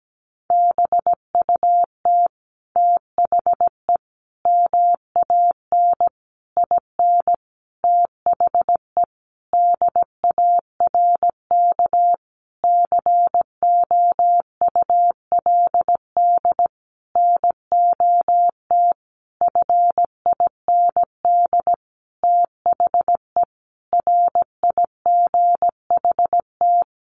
Morse code, heard as a keyed tone, about 17 wpm, 700 Hz. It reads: BUT THE MAN IN THE DARK COULD NOT FIND THE RIGHT
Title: Sentences from Top 600 Words 17wpm